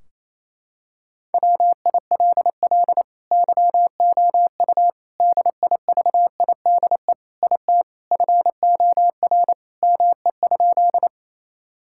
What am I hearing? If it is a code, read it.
Morse code, heard as a keyed tone, about 28 wpm, 700 Hz. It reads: WILL YOU DIVIDE IT FOR ME?
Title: Sentences from Top 1600 Words 28wpm